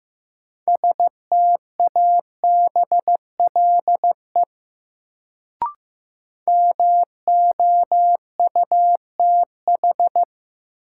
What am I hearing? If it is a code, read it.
Morse code, heard as a keyed tone, about 15 wpm, 700 Hz. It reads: STABLE MOUTH
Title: Sentences from Top 1500 Words - New Word Drill - 15wpm